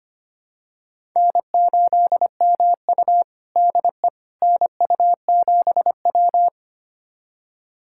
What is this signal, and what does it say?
Morse code, heard as a keyed tone, about 25 wpm, 700 Hz. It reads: N8MU DE NU7W
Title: USA Callsign DE Callsign - 4 Characters - 25wpm